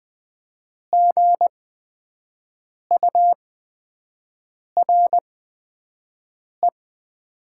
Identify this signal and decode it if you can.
Morse code, heard as a keyed tone, about 20 wpm, spaced as for 5 wpm, 700 Hz. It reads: GURE
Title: 8x Character Spacing - Top 500 Words - 20wpm